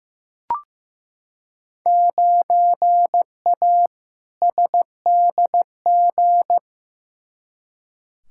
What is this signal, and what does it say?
Morse code, heard as a keyed tone, about 15 wpm, 700 Hz. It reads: 9A SDG
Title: Field Day Exchanges Encore - 15wpm